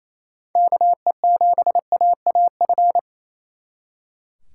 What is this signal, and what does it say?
Morse code, heard as a keyed tone, about 36 wpm, 700 Hz. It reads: KE7AAF